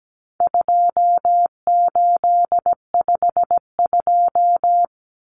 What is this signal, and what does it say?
Morse code, heard as a keyed tone, about 17 wpm, 700 Hz. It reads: 2852
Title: Common Cut Numbers - 4-Digit 17wpm